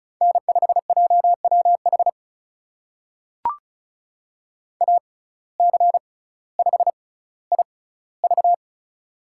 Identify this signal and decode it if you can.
Morse code, heard as a keyed tone, about 36 wpm, spaced as for 18 wpm, 700 Hz. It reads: N5JWH AC5IV
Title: USA Call Signs - 5 Characters - 6x Character Spacing - 35wpm